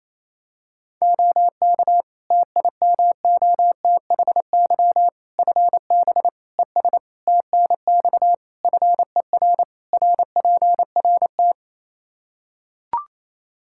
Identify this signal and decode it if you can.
Morse code, heard as a keyed tone, about 28 wpm, 700 Hz. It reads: OK TIMOTHY FB ES TNX FER RPRT
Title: Practice QSO - Exchange 2 - Info - 28wpm